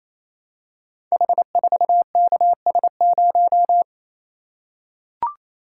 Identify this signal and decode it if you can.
Morse code, heard as a keyed tone, about 29 wpm, 700 Hz. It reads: H4KS0
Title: Code Groups - 5 Letter:Numbers - 28wpm